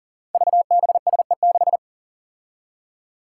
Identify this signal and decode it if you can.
Morse code, heard as a keyed tone, about 40 wpm, 700 Hz. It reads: VBSE6
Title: Five Challenging Letter-Number Combinations 40wpm